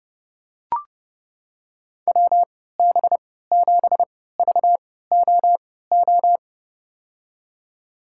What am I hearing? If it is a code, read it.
Morse code, heard as a keyed tone, about 30 wpm, spaced as for 17 wpm, 700 Hz. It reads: WB7VOO
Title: USA Call Signs - 6 Characters - 3x Character Spacing - 30wpm